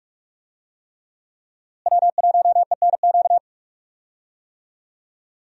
Morse code, heard as a keyed tone, about 41 wpm, 700 Hz. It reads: W1ENQ
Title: USA Call Signs - 5 Characters - Repeated 3x - 45wpm